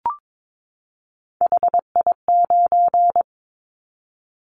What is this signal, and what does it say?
Morse code, heard as a keyed tone, about 22 wpm, 700 Hz. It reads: HI9
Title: Callsign Prefixes for Top 100 DXCC - 22wpm